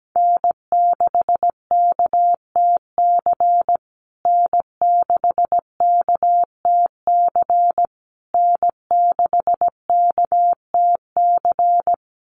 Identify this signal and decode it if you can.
Morse code, heard as a keyed tone, about 17 wpm, 700 Hz. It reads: N6KTC N6KTC N6KTC
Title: USA Call Signs Repeated 3x 17wpm